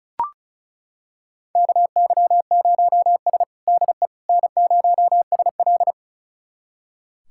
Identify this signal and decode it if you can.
Morse code, heard as a keyed tone, about 36 wpm, 700 Hz. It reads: KY0S DE N0SL
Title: USA Callsign DE Callsign - 4 Characters - 35wpm